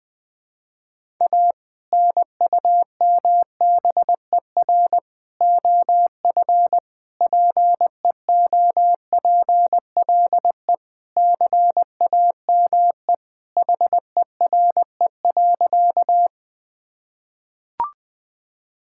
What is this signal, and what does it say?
Morse code, heard as a keyed tone, about 20 wpm, 700 Hz. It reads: A NUMBER OF PEOPLE CAME HERE.
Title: Sentences with 2-Word Phrases - Spoken Beginning - 20wpm